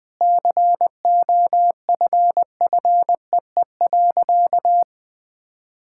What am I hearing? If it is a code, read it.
Morse code, heard as a keyed tone, about 20 wpm, 700 Hz. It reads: COFFEE.
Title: Sentences with 7-Word Phrases - 20wpm